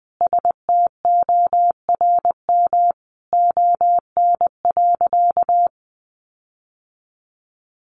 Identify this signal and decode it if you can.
Morse code, heard as a keyed tone, about 20 wpm, 700 Hz. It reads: STORM ON.
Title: Sentences from Top 1500 Words 20wpm